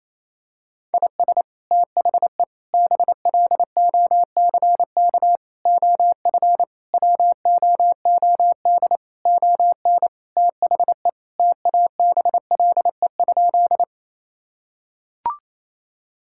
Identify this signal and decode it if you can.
Morse code, heard as a keyed tone, about 28 wpm, 700 Hz. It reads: IS THE BLOCK OF WOOD ON THE TABLE?
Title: Sentences from Top 1600 Words 28wpm